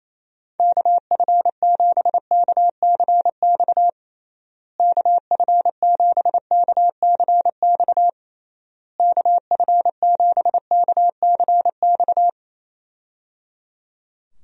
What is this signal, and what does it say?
Morse code, heard as a keyed tone, about 28 wpm, 700 Hz. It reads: KF7KCX KF7KCX KF7KCX
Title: USA Call Signs - 6 Characters - Repeated 3x - 28wpm